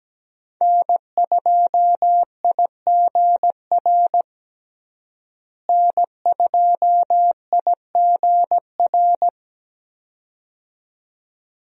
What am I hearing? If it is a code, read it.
Morse code, heard as a keyed tone, about 17 wpm, 700 Hz. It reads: N2IGR N2IGR
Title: USA Call Signs - 5 Characters - Repeated 2x - 17wpm